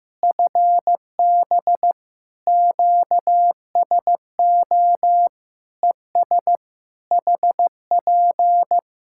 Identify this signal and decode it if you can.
Morse code, heard as a keyed tone, about 15 wpm, 700 Hz. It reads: FB QSO ES HP
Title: Practice QSO - Exchange 4 - Ending - 15wpm